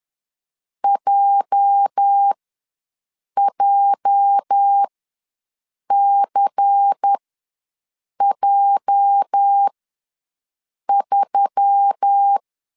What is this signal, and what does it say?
Morse code, heard as a keyed tone, about 10 wpm, spaced as for 6 wpm, 800 Hz. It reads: JJCJ3
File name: LEZIONI / 8 YJ45 / 8 YJ ws 10wpm 4043